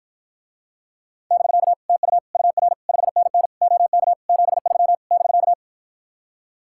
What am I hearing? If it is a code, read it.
Morse code, heard as a keyed tone, about 54 wpm, 700 Hz. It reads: <BK> TU UR 5NN OK 73 <BK>